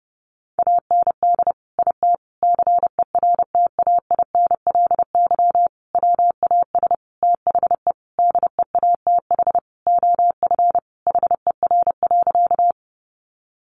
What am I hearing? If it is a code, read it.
Morse code, heard as a keyed tone, about 30 wpm, 700 Hz. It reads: AND IT CERTAINLY WAS THE DEATH OF HER.
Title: Sentences from Top 1100 Words 30wpm